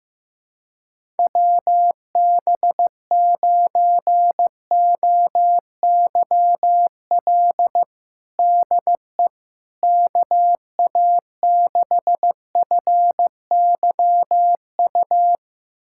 Morse code, heard as a keyed tone, about 15 wpm, 700 Hz. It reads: WB9OYL DE KA6FYU
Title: USA Callsign DE Callsign - 6 Characters - 15wpm